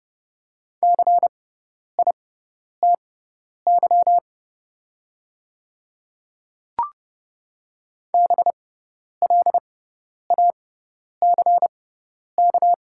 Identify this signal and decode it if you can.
Morse code, heard as a keyed tone, about 29 wpm, spaced as for 10 wpm, 690 Hz. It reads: CITY BLACK
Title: 6x Character Spacing - Top 500 Words - 30wpm